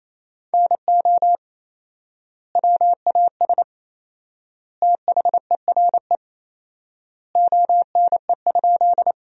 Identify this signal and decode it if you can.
Morse code, heard as a keyed tone, about 28 wpm, 700 Hz. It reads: NO WAS THERE ONE?
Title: 4x Word-Spacing - Sentences from Top 100 Words - 28wpm